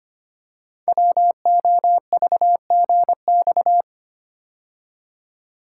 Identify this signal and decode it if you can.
Morse code, heard as a keyed tone, about 25 wpm, 690 Hz. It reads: WOVGX
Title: Code Groups - 5 Letters - 25wpm